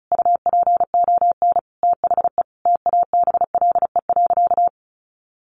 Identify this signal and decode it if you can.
Morse code, heard as a keyed tone, about 34 wpm, 700 Hz. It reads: UPON THE TABLE.
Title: Sentences from Top 1100 Words 35wpm